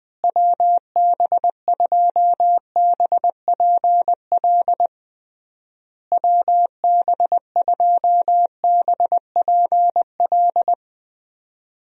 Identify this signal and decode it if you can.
Morse code, heard as a keyed tone, about 20 wpm, 700 Hz. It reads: WB2BPL WB2BPL
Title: USA Call Signs - 6 Characters - Repeated 3x - 20wpm